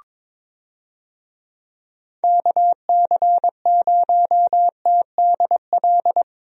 Morse code, heard as a keyed tone, about 22 wpm, 700 Hz. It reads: KC0TDL
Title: USA Call Signs - 6 Characters - 22wpm